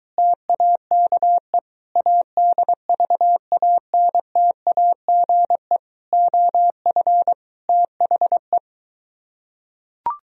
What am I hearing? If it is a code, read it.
Morse code, heard as a keyed tone, about 23 wpm, 700 Hz. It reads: TAKE ADVANTAGE OF THE